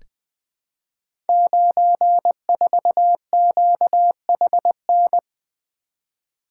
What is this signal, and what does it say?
Morse code, heard as a keyed tone, about 20 wpm, 700 Hz. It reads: 94QHN
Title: Code Groups - 5 Letter:Numbers - 20wpm